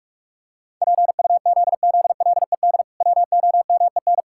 Morse code, heard as a keyed tone, about 46 wpm, 700 Hz. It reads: PUZZLED WOMEN
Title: Pangram Challenge v2 - 45wpm